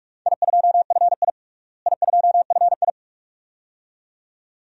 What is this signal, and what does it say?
Morse code, heard as a keyed tone, about 41 wpm, 700 Hz. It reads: I2FI I2FI
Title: DXCC-100 Call Signs - 4 Characters - Repeated 2x - 45wpm